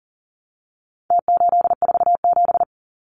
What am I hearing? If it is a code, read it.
Morse code, heard as a keyed tone, about 40 wpm, 700 Hz. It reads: T847
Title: Numbers - Contest Style - 4-Digit 40wpm